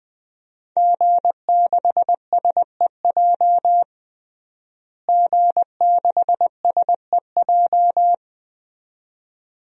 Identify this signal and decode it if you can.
Morse code, heard as a keyed tone, about 20 wpm, 700 Hz. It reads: G6SEJ G6SEJ
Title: DXCC-100 Call Signs - 5 Characters - Repeated 2x - 20wpm